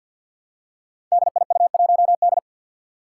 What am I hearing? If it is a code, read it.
Morse code, heard as a keyed tone, about 53 wpm, 700 Hz. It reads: DIU1D
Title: Five Challenging Letter-Number Combinations 50wpm